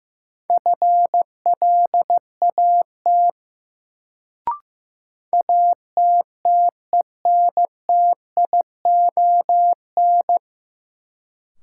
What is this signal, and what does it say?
Morse code, heard as a keyed tone, about 16 wpm, 700 Hz. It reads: FLAT ATTENTION
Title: Sentences from Top 1000 Words - New Word Drill - 15wpm